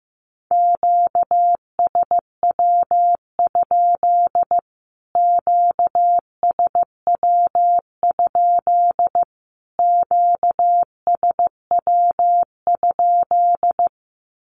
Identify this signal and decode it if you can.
Morse code, heard as a keyed tone, about 15 wpm, 700 Hz. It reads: QSW? QSW? QSW?
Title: Q-Codes Repeated 3x 15wpm